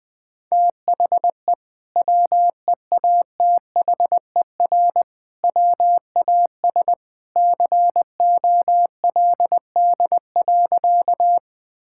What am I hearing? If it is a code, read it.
Morse code, heard as a keyed tone, about 20 wpm, 700 Hz. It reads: THE WEATHER WAS COLD.
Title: Sentences from Top 1000 Words - Review Part 5 - 20wpm